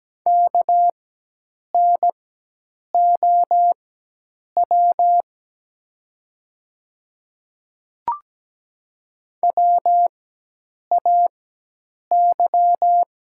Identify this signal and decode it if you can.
Morse code, heard as a keyed tone, about 17 wpm, spaced as for 8 wpm, 700 Hz. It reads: KNOW WAY
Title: 4x Character Spacing - Top 100 Words - 17wpm